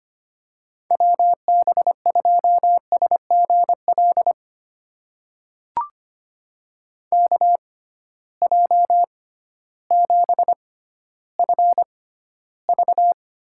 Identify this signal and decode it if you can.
Morse code, heard as a keyed tone, about 25 wpm, spaced as for 13 wpm, 700 Hz. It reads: WB2SGL KJ7FV
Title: USA Call Signs - 6 Characters - 6x Character Spacing - 25wpm